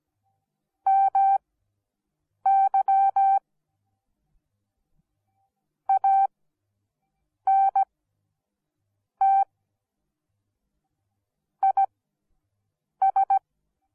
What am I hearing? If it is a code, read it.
Morse code, heard as a keyed tone, about 17 wpm, spaced as for 6 wpm, 780 Hz. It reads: MY ANT IS